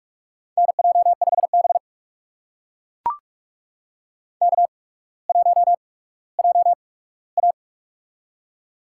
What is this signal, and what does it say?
Morse code, heard as a keyed tone, about 44 wpm, spaced as for 19 wpm, 700 Hz. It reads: NJ5B K1JA
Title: USA Call Signs - 4 Characters - 8x Character Spacing - 45wpm